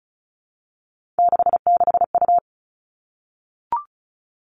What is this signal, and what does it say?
Morse code, heard as a keyed tone, about 36 wpm, 700 Hz. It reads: 66U
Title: Three Challenging Letter-Number Combinations 35wpm